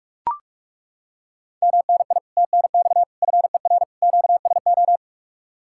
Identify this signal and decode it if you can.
Morse code, heard as a keyed tone, about 36 wpm, 700 Hz. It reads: MNI TNX FER QSO
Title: Speed-Racing Common QSO Phrases Encore - 30wpm